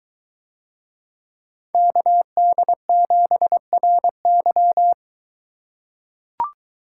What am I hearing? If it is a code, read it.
Morse code, heard as a keyed tone, about 22 wpm, 700 Hz. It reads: KD7RY